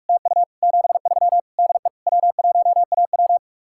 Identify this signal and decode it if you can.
Morse code, heard as a keyed tone, about 56 wpm, 700 Hz. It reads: TU 73 DE W1AW
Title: Speed-Racing Common QSO Phrases Encore - 45wpm